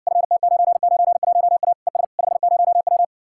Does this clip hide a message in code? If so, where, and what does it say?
Yes: Morse code from 0.1 to 3.0 s, heard as a keyed tone, about 60 wpm, 700 Hz. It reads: FT991A ES 50W